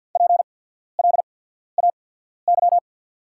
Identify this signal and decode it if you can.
Morse code, heard as a keyed tone, about 48 wpm, spaced as for 14 wpm, 690 Hz. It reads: PLAY
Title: 8x Character Spacing - Top 200 Words - 50wpm